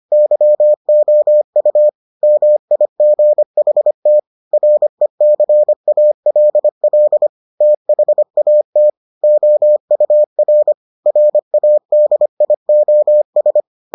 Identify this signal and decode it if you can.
Morse code, heard as a keyed tone, about 25 wpm, 600 Hz. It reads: YOU MIGHT RECALL THAT OUR RADIOS